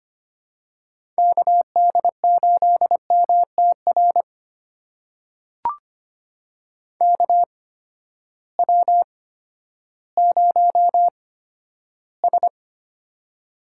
Morse code, heard as a keyed tone, about 25 wpm, spaced as for 12 wpm, 700 Hz. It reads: KD8MTR KW0S
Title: USA Call Signs - 8x Character Spacing - 25wpm